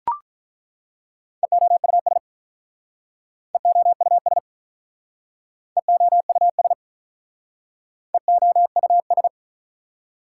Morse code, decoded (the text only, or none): EOUS EOUS EOUS EOUS